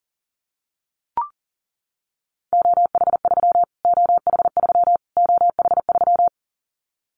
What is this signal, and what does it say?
Morse code, heard as a keyed tone, about 40 wpm, 700 Hz. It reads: OH3 OH3 OH3
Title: Callsign Prefixes for Top 25 DXCC - Repeated 3x - 40wpm